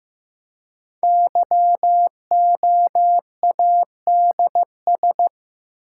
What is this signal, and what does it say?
Morse code, heard as a keyed tone, about 15 wpm, 700 Hz. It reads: YOADS
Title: Code Groups - 5 Letter:Numbers - 15wpm